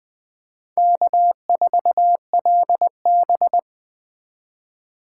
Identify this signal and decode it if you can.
Morse code, heard as a keyed tone, about 21 wpm, 700 Hz. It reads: K4LB
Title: Speed Racing - USA 1x2 Call Signs 17wpm